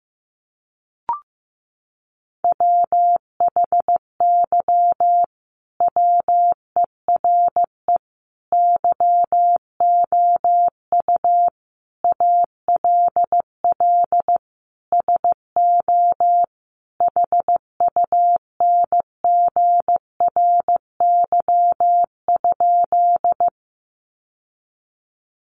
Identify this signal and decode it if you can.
Morse code, heard as a keyed tone, about 14 wpm, 700 Hz. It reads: WHY WERE YOU ALL SO HUNGRY?
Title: Sentences from Top 1400 Words 15wpm